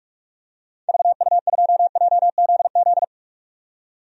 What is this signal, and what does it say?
Morse code, heard as a keyed tone, about 46 wpm, 700 Hz. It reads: VU2JZZ
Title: DXCC-100 Call Signs - 6 Characters - 45wpm